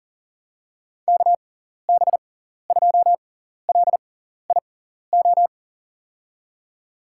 Morse code, heard as a keyed tone, about 40 wpm, spaced as for 14 wpm, 700 Hz. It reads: KB2LIO
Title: USA Call Signs - 6 Characters - 6x Character Spacing - 40wpm